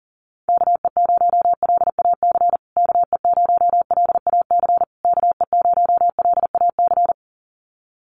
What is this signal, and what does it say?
Morse code, heard as a keyed tone, about 40 wpm, 700 Hz. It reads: KE0LAC KE0LAC KE0LAC
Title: USA Call Signs Repeated 3x 40wpm